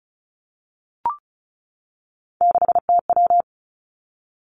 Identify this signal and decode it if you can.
Morse code, heard as a keyed tone, about 36 wpm, 700 Hz. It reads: 6TW
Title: Three Challenging Letter-Number Combinations 35wpm